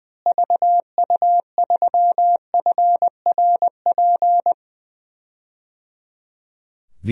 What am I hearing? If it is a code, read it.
Morse code, heard as a keyed tone, about 20 wpm, 700 Hz. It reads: VU3FRP
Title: DXCC-100 Call Signs - 6 Characters - Repeated 3x - 20wpm